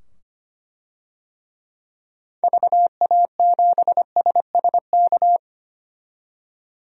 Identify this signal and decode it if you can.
Morse code, heard as a keyed tone, about 25 wpm, 700 Hz. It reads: VA7SSK
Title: DXCC-100 Call Signs - 6 Characters - 25wpm